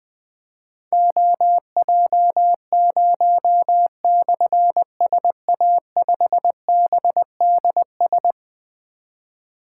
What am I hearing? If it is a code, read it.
Morse code, heard as a keyed tone, about 20 wpm, 700 Hz. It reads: OJ0/SA5BDS
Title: Difficult Call Signs 20wpm